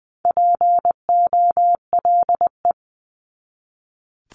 Morse code, heard as a keyed tone, about 20 wpm, 700 Hz. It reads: POLE
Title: 260 Lesson 42e - Words 20-20wpm